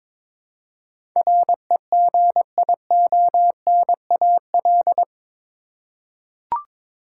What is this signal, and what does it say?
Morse code, heard as a keyed tone, about 22 wpm, 700 Hz. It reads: REGIONAL
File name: practice / 22wpm / Eight Letter Words - 22wpm